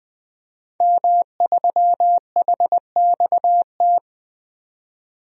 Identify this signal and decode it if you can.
Morse code, heard as a keyed tone, about 20 wpm, 700 Hz. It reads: M3HXT
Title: DXCC-100 Call Signs - 5 Characters - 20wpm